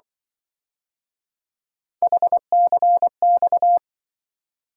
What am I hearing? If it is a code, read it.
Morse code, heard as a keyed tone, about 24 wpm, 700 Hz. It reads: HCX
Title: Speed-Racing - Three Letter-Number Combinations - 20wpm